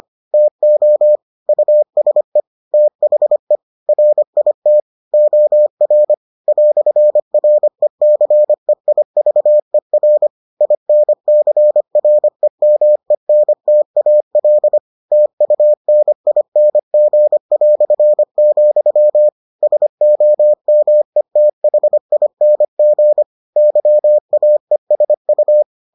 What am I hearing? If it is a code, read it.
Morse code, heard as a keyed tone, about 25 wpm, 600 Hz. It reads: TO USE THE RIT OR "RECEIVER INCREMENTAL TUNING", SOMETHING YAESU